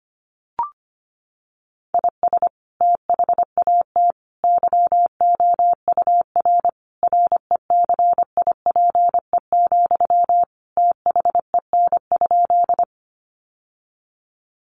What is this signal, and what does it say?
Morse code, heard as a keyed tone, about 25 wpm, 700 Hz. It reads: IS THAT YOUR RECIPE, THEN?
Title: Sentences from Top 1200 Words 25wpm